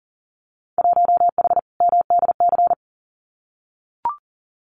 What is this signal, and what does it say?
Morse code, heard as a keyed tone, about 40 wpm, 700 Hz. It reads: 1H MDC
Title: Winter Field Day Exchanges 40wpm